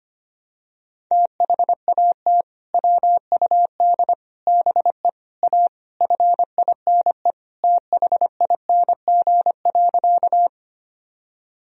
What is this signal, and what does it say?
Morse code, heard as a keyed tone, about 25 wpm, 700 Hz. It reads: THAT WUD BE A FINE THING.
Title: Sentences from Top 500 Words and Abbreviations 25wpm